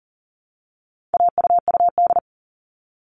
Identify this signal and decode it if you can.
Morse code, heard as a keyed tone, about 40 wpm, 700 Hz. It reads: AUUD
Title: All Cut Numbers - 4-Digit 40wpm